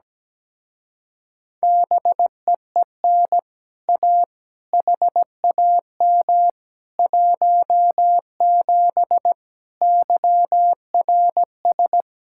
Morse code, heard as a keyed tone, about 17 wpm, 700 Hz. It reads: BEEN A HAM 17 YRS
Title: Speed-Racing Common QSO Phrases - 17wpm